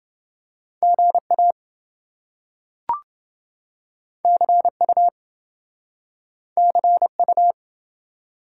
Morse code, heard as a keyed tone, about 27 wpm, 700 Hz. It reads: GA CU CU